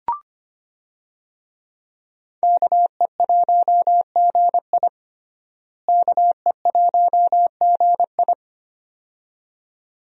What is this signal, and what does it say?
Morse code, heard as a keyed tone, about 25 wpm, 700 Hz. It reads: KE1GI KE1GI
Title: USA Call Signs - 5 Characters - Repeated 2x - 25wpm